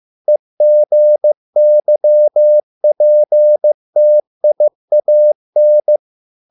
Morse code, heard as a keyed tone, about 15 wpm, 600 Hz. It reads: EGYPTIAN